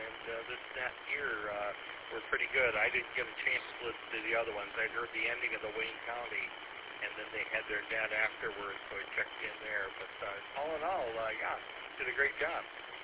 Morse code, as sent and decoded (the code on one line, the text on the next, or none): none